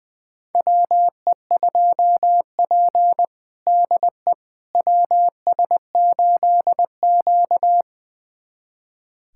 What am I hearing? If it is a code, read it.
Morse code, heard as a keyed tone, about 20 wpm, 700 Hz. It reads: WE2P DE WS8Q
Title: USA Callsign DE Callsign - 4 Characters - 20wpm